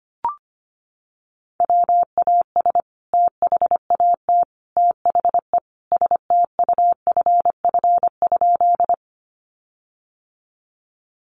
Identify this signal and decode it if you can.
Morse code, heard as a keyed tone, about 25 wpm, 700 Hz. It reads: WAS THAT THE STUFF?
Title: Sentences from Top 1400 Words 25wpm